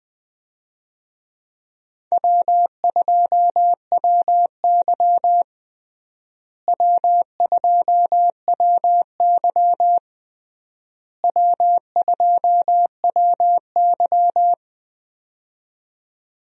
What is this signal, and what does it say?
Morse code, heard as a keyed tone, about 20 wpm, 700 Hz. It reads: W2WY W2WY W2WY
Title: USA Call Signs - 4 Characters - Repeated 3x - 20wpm